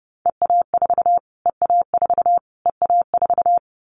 Morse code, heard as a keyed tone, about 28 wpm, 700 Hz. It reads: EA4 EA4 EA4
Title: Callsign Prefixes for Top 25 DXCC - Repeated 3x - 30wpm